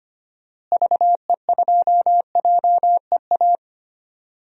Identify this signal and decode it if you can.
Morse code, heard as a keyed tone, about 25 wpm, 700 Hz. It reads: VE2JEA
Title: DXCC-100 Call Signs - 6 Characters - Repeated 2x - 25wpm